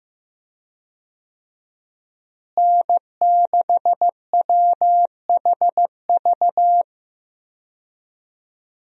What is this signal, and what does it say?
Morse code, heard as a keyed tone, about 15 wpm, 700 Hz. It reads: N6WHV
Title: USA Call Signs - 5 Characters - Repeated 2x - 15wpm